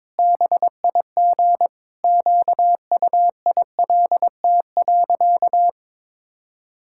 Morse code, heard as a keyed tone, about 22 wpm, 700 Hz. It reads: BIG QUILT.
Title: Pangram Challenge v2 - 22wpm